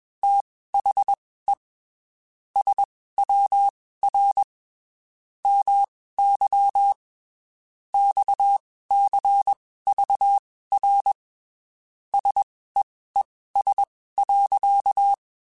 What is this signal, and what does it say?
Morse code, heard as a keyed tone, about 21 wpm, spaced as for 15 wpm, 800 Hz. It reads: THE SWR MY XCVR SEES.
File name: QSOs / QSOs-mp3s / WD0FBT 1@20